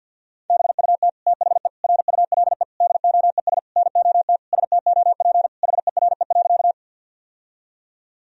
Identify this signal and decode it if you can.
Morse code, heard as a keyed tone, about 50 wpm, 700 Hz. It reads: BUT THE RULE DOES NOT STOP HERE.
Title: Sentences from Top 1000 Words - Review Part 2 - 50wpm